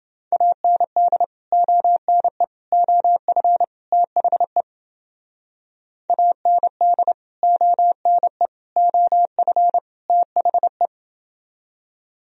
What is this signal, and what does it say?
Morse code, heard as a keyed tone, about 28 wpm, 700 Hz. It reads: AND ONE OF THE AND ONE OF THE